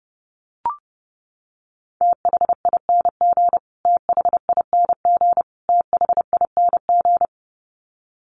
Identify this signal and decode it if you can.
Morse code, heard as a keyed tone, about 30 wpm, 700 Hz. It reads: THING THING THING
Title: Top 300 Words Repeated 30wpm